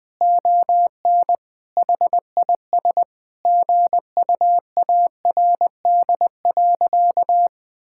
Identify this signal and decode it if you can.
Morse code, heard as a keyed tone, about 20 wpm, 700 Hz. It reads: ON HIS GUARD.
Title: Sentences from Top 1000 Words - Review Part 5 - 20wpm